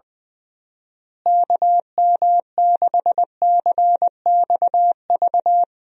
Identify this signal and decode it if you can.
Morse code, heard as a keyed tone, about 20 wpm, 700 Hz. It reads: KM6CXV
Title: USA Call Signs - 6 Characters - Repeated 3x - 20wpm